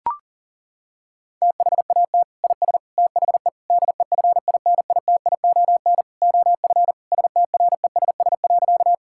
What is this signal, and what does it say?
Morse code, heard as a keyed tone, about 40 wpm, 700 Hz. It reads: THAT IS THE DEFINITION OF STRESS.
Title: Sentences from Top 1000 Words - Review Part 6 - 40wpm